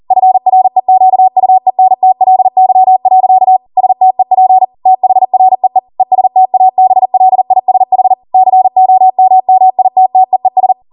Morse code, heard as a keyed tone, about 40 wpm, 750 Hz. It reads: FREQUENTLY. STEP THREE ESTABLISH COMMITTEES